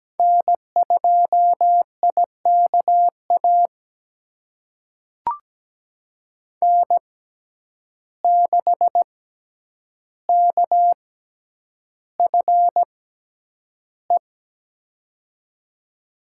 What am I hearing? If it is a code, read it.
Morse code, heard as a keyed tone, about 17 wpm, spaced as for 9 wpm, 700 Hz. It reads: N2IKA N6KFE
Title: USA Call Signs - 5 Characters - 6x Character Spacing - 17wpm